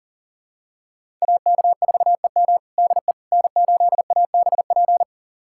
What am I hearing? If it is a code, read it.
Morse code, heard as a keyed tone, about 39 wpm, 700 Hz. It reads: AK4EM DE N8ABP
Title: USA Callsign DE Callsign - 5 Characters - 40wpm